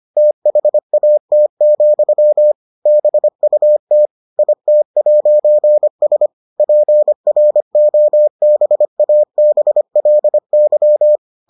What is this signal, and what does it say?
Morse code, heard as a keyed tone, about 25 wpm, 600 Hz. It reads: THAT, BUT IT'S PROBABLY